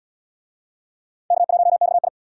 Morse code, heard as a keyed tone, about 71 wpm, 700 Hz. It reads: K1XI